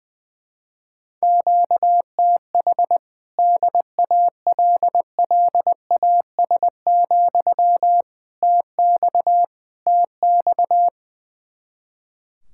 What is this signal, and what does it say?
Morse code, heard as a keyed tone, about 21 wpm, 700 Hz. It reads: QTH DALLAS, TX TX
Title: Speed-Racing Common QSO Phrases - 20wpm